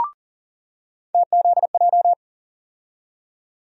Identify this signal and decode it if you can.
Morse code, heard as a keyed tone, about 40 wpm, 700 Hz. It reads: TZJ
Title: USA Call Sign Postfixes - 3 Letters 40wpm